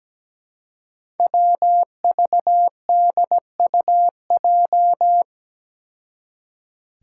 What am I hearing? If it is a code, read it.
Morse code, heard as a keyed tone, about 17 wpm, 700 Hz. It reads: WVDUJ
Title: Five Challenging Letter-Number Combinations 17wpm